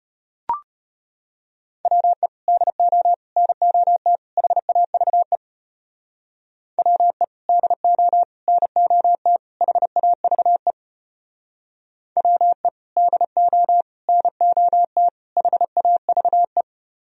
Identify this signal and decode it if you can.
Morse code, heard as a keyed tone, about 30 wpm, 700 Hz. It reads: WE DO NOT HAVE WE DO NOT HAVE WE DO NOT HAVE
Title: Speed-Racing - Sets of 4 Words 25wpm